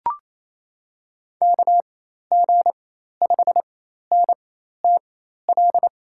Morse code, heard as a keyed tone, about 28 wpm, spaced as for 13 wpm, 700 Hz. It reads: KG5NTL
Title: USA Call Signs - 6 Characters - 4x Character Spacing - 28wpm